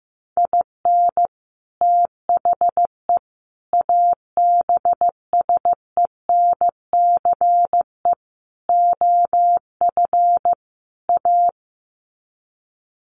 Morse code, heard as a keyed tone, about 15 wpm, 700 Hz. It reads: IN THE ABSENCE OF A
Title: Sets of 5 Words Encore 15wpm